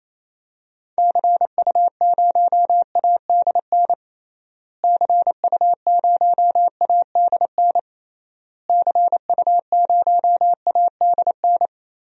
Morse code, heard as a keyed tone, about 28 wpm, 700 Hz. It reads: CU0ADN CU0ADN CU0ADN
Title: DXCC-100 Call Signs - 6 Characters - Repeated 3x - 28wpm